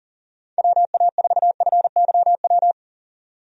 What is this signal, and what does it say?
Morse code, heard as a keyed tone, about 40 wpm, 700 Hz. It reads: WA4FYW
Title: USA Call Signs - 6 Characters - 40wpm